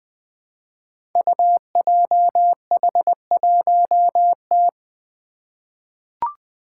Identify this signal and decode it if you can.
Morse code, heard as a keyed tone, about 20 wpm, 700 Hz. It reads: UJH1T
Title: Five Challenging Letter-Number Combinations 20wpm